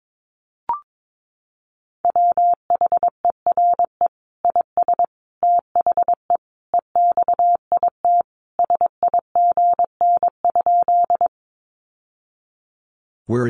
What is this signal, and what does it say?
Morse code, heard as a keyed tone, about 22 wpm, 700 Hz. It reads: WHERE IS THE EXIT SIGN?
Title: Sentences from Top 1100 Words 22wpm